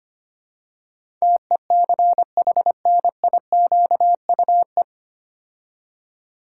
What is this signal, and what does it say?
Morse code, heard as a keyed tone, about 25 wpm, 700 Hz. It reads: TECHNIQUE
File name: practice / 25wpm / 9 Letter Words - First 3 Letters Given - 25wpm